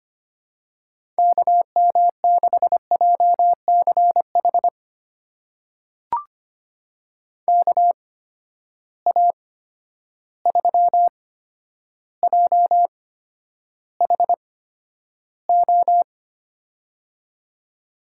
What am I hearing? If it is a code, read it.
Morse code, heard as a keyed tone, about 25 wpm, spaced as for 11 wpm, 700 Hz. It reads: KM6JCH KA3JHO
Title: USA Call Signs - 6 Characters - 8x Character Spacing - 25wpm